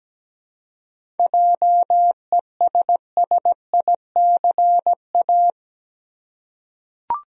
Morse code, heard as a keyed tone, about 17 wpm, 700 Hz. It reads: JESSICA
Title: USA Common Names - 17wpm